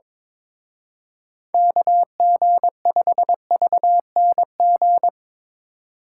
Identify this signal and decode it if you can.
Morse code, heard as a keyed tone, about 23 wpm, 700 Hz. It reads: KG5VNG
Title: Speed Racing - USA 2x3 Call Signs 22wpm